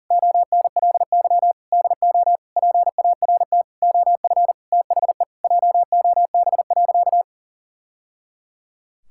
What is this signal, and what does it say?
Morse code, heard as a keyed tone, about 40 wpm, 700 Hz. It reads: ONLY DO PART OF THE JOB.
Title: Sentences from Top 1000 Words - Review Part 5 - 40wpm